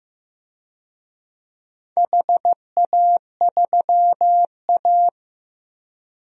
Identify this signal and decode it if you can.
Morse code, heard as a keyed tone, about 15 wpm, 700 Hz. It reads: HA3A
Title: DXCC-100 Call Signs - 4 Characters - 15wpm